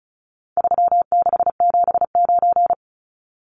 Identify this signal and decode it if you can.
Morse code, heard as a keyed tone, about 34 wpm, 700 Hz. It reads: 3679
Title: Numbers - 4-Digit 35wpm